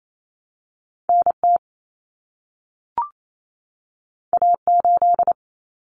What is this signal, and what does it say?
Morse code, heard as a keyed tone, about 28 wpm, 700 Hz. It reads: NT A8
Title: Common Cut Numbers - 2-Digit 28wpm